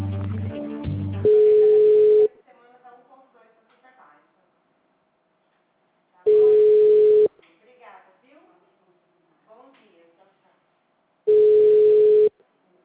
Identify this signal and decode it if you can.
Ringback tone (Europe, 425 Hz), with speech.
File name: out- -2005- - - .167